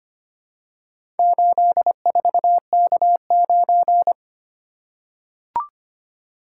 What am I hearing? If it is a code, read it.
Morse code, heard as a keyed tone, about 25 wpm, 700 Hz. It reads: 84K9
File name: practice / 25wpm / Four Letter-Number Combinations - 25wpm